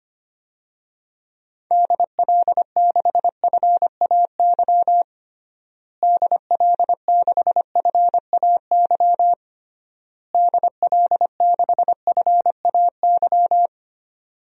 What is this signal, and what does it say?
Morse code, heard as a keyed tone, about 25 wpm, 700 Hz. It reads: DL6FAY DL6FAY DL6FAY
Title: DXCC-100 Call Signs - 6 Characters - Repeated 3x - 25wpm